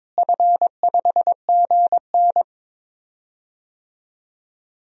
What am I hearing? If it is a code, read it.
Morse code, heard as a keyed tone, about 22 wpm, 700 Hz. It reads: F5GN
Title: DXCC-100 Call Signs - 4 Characters - Repeated 3x - 22wpm